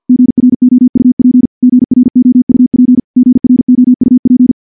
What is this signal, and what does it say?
Morse code, heard as a keyed tone, about 51 wpm, 250 Hz. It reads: GROUP GROUP GROUP
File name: morse / morse-training-50WPM